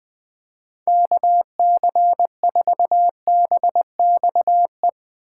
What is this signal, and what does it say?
Morse code, heard as a keyed tone, about 20 wpm, 700 Hz. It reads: KC4BXE
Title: Speed Racing - USA 2x3 Call Signs 17wpm